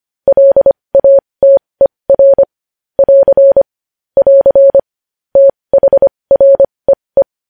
Morse code, heard as a keyed tone, about 25 wpm, spaced as for 20 wpm, 550 Hz. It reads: LATER <AR> <AR> THREE
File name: podcast / News headlines-1x-25x20